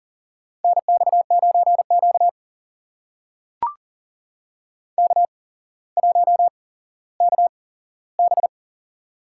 Morse code, heard as a keyed tone, about 40 wpm, spaced as for 17 wpm, 700 Hz. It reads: NX9Q K1KB
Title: USA Call Signs - 4 Characters - 8x Character Spacing - 40wpm